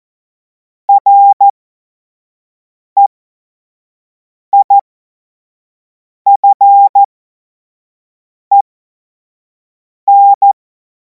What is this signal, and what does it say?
Morse code, heard as a keyed tone, about 14 wpm, spaced as for 5 wpm, 800 Hz. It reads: REIFEN
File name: Morsekurs / Witze / Witz 07. April 2026 25 F70